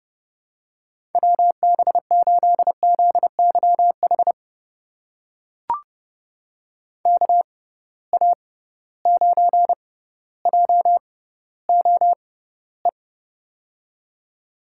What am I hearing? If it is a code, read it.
Morse code, heard as a keyed tone, about 30 wpm, spaced as for 15 wpm, 700 Hz. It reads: WB8ZYH KA9JOE
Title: USA Call Signs - 6 Characters - 6x Character Spacing - 30wpm